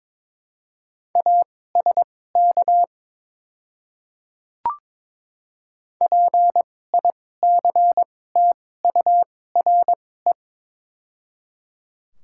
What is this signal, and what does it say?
Morse code, heard as a keyed tone, about 22 wpm, spaced as for 16 wpm, 700 Hz. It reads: ASK PICTURE
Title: 2x Character Spacing - Top 200 Words - 22wpm